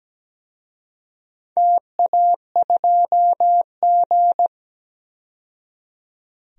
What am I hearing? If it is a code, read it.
Morse code, heard as a keyed tone, about 17 wpm, 700 Hz. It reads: TA2G
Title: DXCC-100 Call Signs - 4 Characters - 17wpm